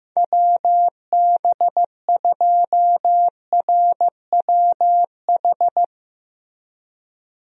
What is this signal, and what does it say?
Morse code, heard as a keyed tone, about 15 wpm, 700 Hz. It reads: WB2RWH
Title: USA Call Signs - 6 Characters - Repeated 3x - 15wpm